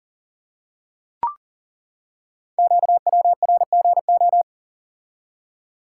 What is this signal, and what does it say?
Morse code, heard as a keyed tone, about 41 wpm, 700 Hz. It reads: QWRGO
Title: Code Groups - 5 Letters - 40wpm